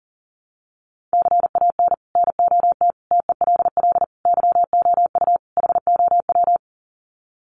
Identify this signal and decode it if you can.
Morse code, heard as a keyed tone, about 40 wpm, 700 Hz. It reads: CAN NOT TELL YOU HOW
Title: Sets of 5 Words Encore2 40wpm